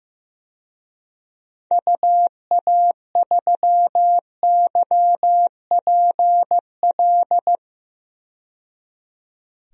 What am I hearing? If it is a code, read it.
Morse code, heard as a keyed tone, about 15 wpm, 690 Hz. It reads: UA3YPL
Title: DXCC-100 Call Signs - 6 Characters - 15wpm